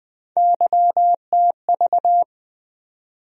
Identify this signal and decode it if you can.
Morse code, heard as a keyed tone, about 20 wpm, 700 Hz. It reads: YTV
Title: USA Call Sign Postfixes - 3 Letters 20wpm